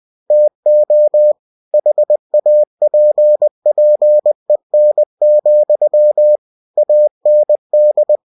Morse code, heard as a keyed tone, about 20 wpm, 600 Hz. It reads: TO HAPPEN, AND
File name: cw-books / winnie-the-pooh / winnie-20wpm-page-0076